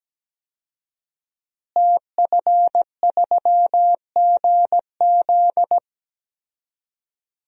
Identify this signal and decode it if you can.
Morse code, heard as a keyed tone, about 17 wpm, 700 Hz. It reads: TF3GZ
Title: DXCC-100 Call Signs - 5 Characters - Repeated 3x - 17wpm